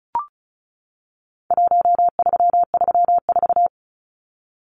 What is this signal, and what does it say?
Morse code, heard as a keyed tone, about 34 wpm, 700 Hz. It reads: 1334
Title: Numbers - Contest Style - 4-Digit 35wpm